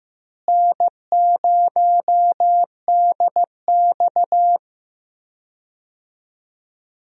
Morse code, heard as a keyed tone, about 15 wpm, 700 Hz. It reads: N0DX
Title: USA Call Signs - 4 Characters - Repeated 2x - 15wpm